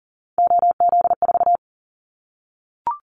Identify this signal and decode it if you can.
Morse code, heard as a keyed tone, about 40 wpm, 700 Hz. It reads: OZ4
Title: Callsign Prefixes for Top 100 DXCC - 40wpm